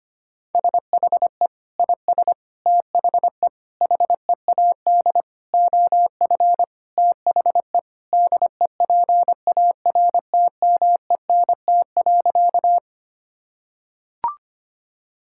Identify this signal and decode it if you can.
Morse code, heard as a keyed tone, about 25 wpm, 700 Hz. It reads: SHE IS THE HEAD OF THE DEPARTMENT.
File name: practice / 25wpm / Sentences with 4-Word Phrases - 25wpm